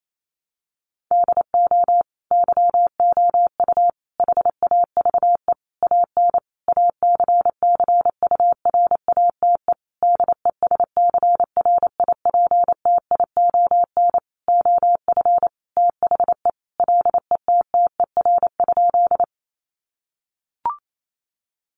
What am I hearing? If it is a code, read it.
Morse code, heard as a keyed tone, about 28 wpm, 700 Hz. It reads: DO YOU HAVE AN ACCURATE DESCRIPTION OF THE LETTER?
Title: Sentences from Top 1300 Words 28wpm